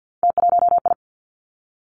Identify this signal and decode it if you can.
Morse code, heard as a keyed tone, about 50 wpm, 700 Hz. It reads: T1I
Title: Three Challenging Letter-Number Combinations 50wpm